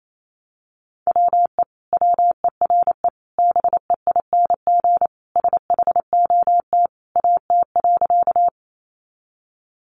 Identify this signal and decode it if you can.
Morse code, heard as a keyed tone, about 28 wpm, 700 Hz. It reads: WE WERE BEING SHOT AT.
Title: Sentences from Top 1100 Words 28wpm